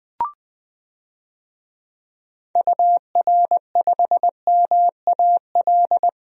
Morse code, heard as a keyed tone, about 20 wpm, 700 Hz. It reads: UR5MAL
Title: DXCC-100 Call Signs - 6 Characters - Repeated 3x - 20wpm